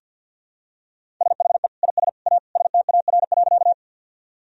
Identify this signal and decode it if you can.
Morse code, heard as a keyed tone, about 50 wpm, 700 Hz. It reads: SHE IS A STAR.
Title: Sentences from Top 1000 Words - Review Part 2 - 50wpm